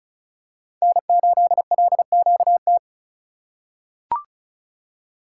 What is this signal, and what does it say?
Morse code, heard as a keyed tone, about 36 wpm, 700 Hz. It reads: N8LQT
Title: USA Call Signs - 5 Characters - 2x Character Spacing - 35wpm